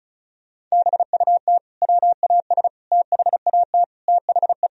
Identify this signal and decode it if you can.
Morse code, heard as a keyed tone, about 36 wpm, 700 Hz. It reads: BUT WAS THAT THE
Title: Sentences from Top 1000 Words - Review Part 2 - 35wpm